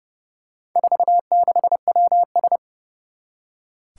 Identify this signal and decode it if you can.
Morse code, heard as a keyed tone, about 29 wpm, 700 Hz. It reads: 46WS
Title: Four Challenging Letter-Number Combinations 30wpm